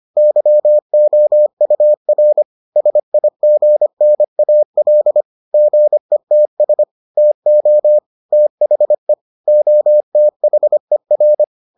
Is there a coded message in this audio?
Yes: Morse code from 0.2 to 11.4 s, heard as a keyed tone, about 25 wpm, 600 Hz. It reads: YOUR SIGNAL GETS TO THE OTHER